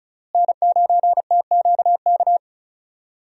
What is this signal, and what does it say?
Morse code, heard as a keyed tone, about 34 wpm, 700 Hz. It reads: N9TQK
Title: USA Call Signs - 5 Characters - Repeated 3x - 35wpm